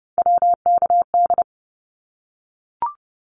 Morse code, heard as a keyed tone, about 30 wpm, 700 Hz. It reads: WKD
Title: Abbreviations Repeated 3x - 30wpm